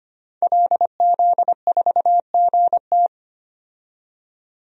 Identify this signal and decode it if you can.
Morse code, heard as a keyed tone, about 25 wpm, 700 Hz. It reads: LZ4GT
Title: DXCC-100 Call Signs - 5 Characters - 25wpm